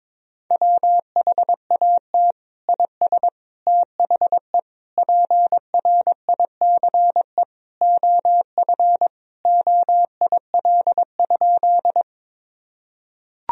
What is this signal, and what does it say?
Morse code, heard as a keyed tone, about 22 wpm, 700 Hz. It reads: WHAT IS THE PRICE OF OIL?
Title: Sentences from Top 1000 Words - Review Part 4 - 22wpm